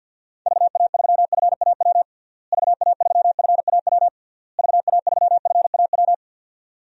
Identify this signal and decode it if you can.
Morse code, heard as a keyed tone, about 52 wpm, 700 Hz. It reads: VA3FAW VA3FAW VA3FAW
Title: DXCC-100 Call Signs - 6 Characters - Repeated 3x - 50wpm